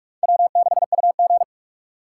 Morse code, heard as a keyed tone, about 46 wpm, 700 Hz. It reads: W6UG
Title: USA Call Signs - 4 Characters - 2x Character Spacing - 45wpm